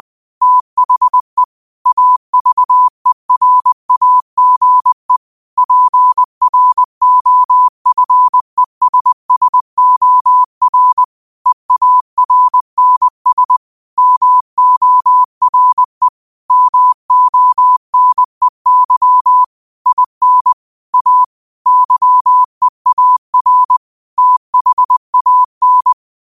Morse code, heard as a keyed tone, about 20 wpm, 1000 Hz. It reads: THE AVERAGE PROFESSOR EARNS MORE MONEY IN A YEAR THAN
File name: qotdfiles / 20wpm